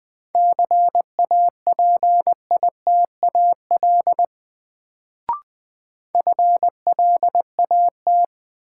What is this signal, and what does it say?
Morse code, heard as a keyed tone, about 20 wpm, 700 Hz. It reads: CAPITAL FLAT
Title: Sentences from Top 1000 Words - New Word Drill - 20wpm